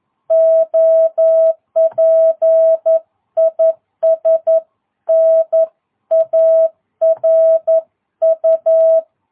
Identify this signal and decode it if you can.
Morse code, heard as a keyed tone, about 11 wpm, 660 Hz. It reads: OP IS NARU
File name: jh8cbh / 2022cw